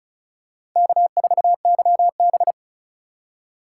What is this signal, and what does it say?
Morse code, heard as a keyed tone, about 36 wpm, 700 Hz. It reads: K4YB
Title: USA Call Signs - 2x Character Spacing - 35wpm